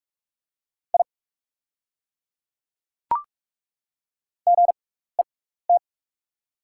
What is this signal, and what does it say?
Morse code, heard as a keyed tone, about 42 wpm, spaced as for 15 wpm, 700 Hz. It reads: I GET
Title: 6x Character Spacing - Top 500 Words - 45wpm